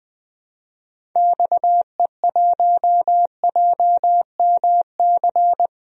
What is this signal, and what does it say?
Morse code, heard as a keyed tone, about 20 wpm, 700 Hz. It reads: XE1JMC
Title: DXCC-100 Call Signs - 6 Characters - Repeated 2x - 20wpm